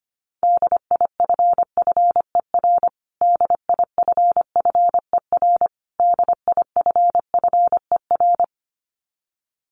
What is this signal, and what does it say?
Morse code, heard as a keyed tone, about 25 wpm, 700 Hz. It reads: DIFFER DIFFER DIFFER
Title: Top 300 Words Repeated 25wpm